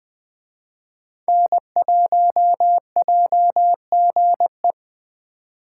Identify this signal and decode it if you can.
Morse code, heard as a keyed tone, about 20 wpm, 700 Hz. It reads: N1JGE
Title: Speed Racing - USA 1x3 Call Signs 17wpm